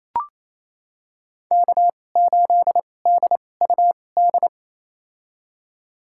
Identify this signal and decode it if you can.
Morse code, heard as a keyed tone, about 28 wpm, spaced as for 20 wpm, 700 Hz. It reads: K8DUD
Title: USA Call Signs - 5 Characters - 2x Character Spacing - 28wpm